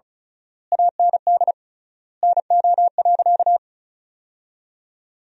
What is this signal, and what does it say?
Morse code, heard as a keyed tone, about 34 wpm, 700 Hz. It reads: AND NO.
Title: 3x Word-Spacing - Sentences from Top 500 Words - 35wpm